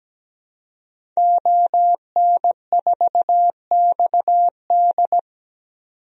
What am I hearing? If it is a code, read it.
Morse code, heard as a keyed tone, about 17 wpm, 700 Hz. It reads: ON4XD
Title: DXCC-100 Call Signs - 5 Characters - 17wpm